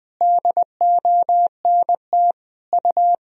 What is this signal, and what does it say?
Morse code, heard as a keyed tone, about 20 wpm, 700 Hz. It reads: DONT U
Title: Sentences from Top 300 Words and Abbreviations 20wpm